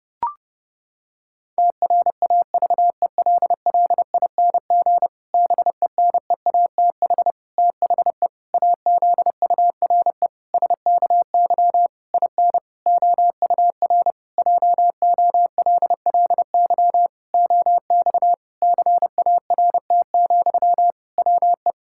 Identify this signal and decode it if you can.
Morse code, heard as a keyed tone, about 30 wpm, 700 Hz. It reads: TRAVELLING BENEATH THE AZURE SKY IN OUR JOLLY OX CART, WE
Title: Pangram Challenge v2 - 30wpm